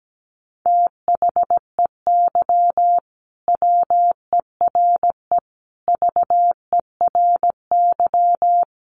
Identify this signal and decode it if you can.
Morse code, heard as a keyed tone, about 17 wpm, 700 Hz. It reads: THEY WERE VERY
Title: Sentences from Top 1000 Words 17wpm